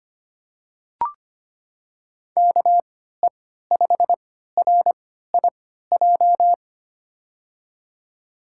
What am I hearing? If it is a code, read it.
Morse code, heard as a keyed tone, about 25 wpm, spaced as for 14 wpm, 700 Hz. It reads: KE5RIJ
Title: USA Call Signs - 6 Characters - 3x Character Spacing - 25wpm